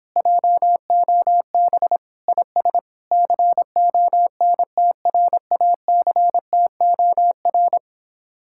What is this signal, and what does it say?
Morse code, heard as a keyed tone, about 26 wpm, 700 Hz. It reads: JOB IS CONTRACTOR